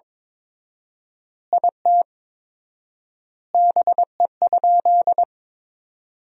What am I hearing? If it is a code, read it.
Morse code, heard as a keyed tone, about 22 wpm, 700 Hz. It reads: IT BE?
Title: 4x Word-Spacing - Sentences from Top 100 Words - 22wpm